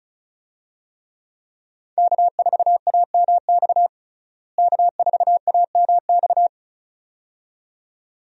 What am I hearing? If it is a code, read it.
Morse code, heard as a keyed tone, about 36 wpm, 700 Hz. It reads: K4AMX K4AMX
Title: USA Call Signs - 5 Characters - Repeated 2x - 35wpm